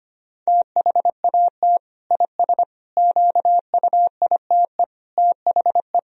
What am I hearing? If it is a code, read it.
Morse code, heard as a keyed tone, about 25 wpm, 700 Hz. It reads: THAT IS QUITE THE
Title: Sentences from Top 1000 Words - Review Part 1 - 25wpm